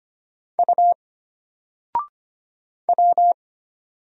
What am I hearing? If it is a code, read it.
Morse code, heard as a keyed tone, about 25 wpm, 700 Hz. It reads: U W
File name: Single-Letter-Number-25wpm